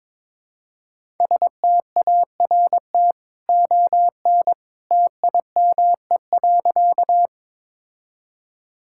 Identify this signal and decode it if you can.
Morse code, heard as a keyed tone, about 22 wpm, 700 Hz. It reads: START ON TIME.
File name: practice / 22wpm / Easy Sentences with Hints - 22wpm